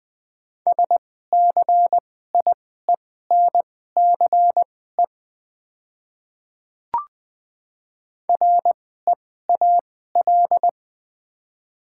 Morse code, heard as a keyed tone, about 20 wpm, spaced as for 15 wpm, 700 Hz. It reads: SCIENCE REAL
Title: 2x Character Spacing - Top 500 Words - 20wpm